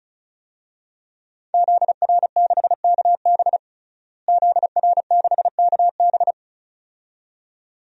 Morse code, heard as a keyed tone, about 34 wpm, 700 Hz. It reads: ZR6KB ZR6KB
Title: DXCC-100 Call Signs - 5 Characters - Repeated 2x - 35wpm